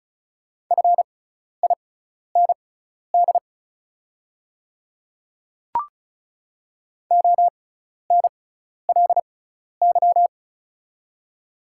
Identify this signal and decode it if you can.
Morse code, heard as a keyed tone, about 36 wpm, spaced as for 12 wpm, 700 Hz. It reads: FIND ONLY
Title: 6x Character Spacing - Top 200 Words - 35wpm